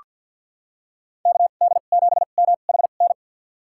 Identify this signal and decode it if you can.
Morse code, heard as a keyed tone, about 48 wpm, spaced as for 35 wpm, 700 Hz. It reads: KD7MHN
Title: USA Call Signs - 2x Character Spacing - 50wpm